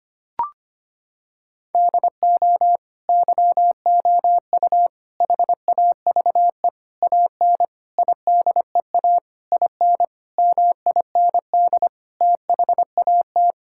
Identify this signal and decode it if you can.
Morse code, heard as a keyed tone, about 25 wpm, 700 Hz. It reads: DO YOU HAVE AN IDEA IN MIND THAT
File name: Sets of 9 Words - 25wpm